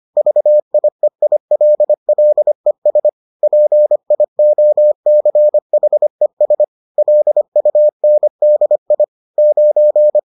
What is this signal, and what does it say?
Morse code, heard as a keyed tone, about 25 wpm, 600 Hz. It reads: VIEILLES PIOCHES LUNDI 9
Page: bulletin UFT sous forme texte et en CW